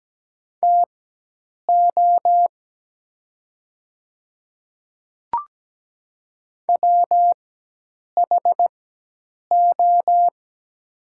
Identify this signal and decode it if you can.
Morse code, heard as a keyed tone, about 17 wpm, spaced as for 8 wpm, 700 Hz. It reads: TO WHO
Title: 4x Character Spacing - Top 100 Words - 17wpm